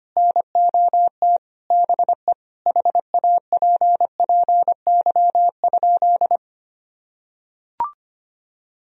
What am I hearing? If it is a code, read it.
Morse code, heard as a keyed tone, about 25 wpm, 700 Hz. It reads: NOT BE HAPPY?
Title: Sentences from Top 1000 Words - Review Part 2 - 25wpm